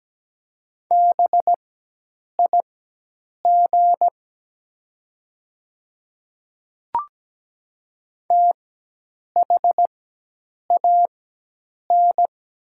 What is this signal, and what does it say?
Morse code, heard as a keyed tone, about 17 wpm, spaced as for 8 wpm, 700 Hz. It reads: BIG THAN
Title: 4x Character Spacing - Top 200 Words - 17wpm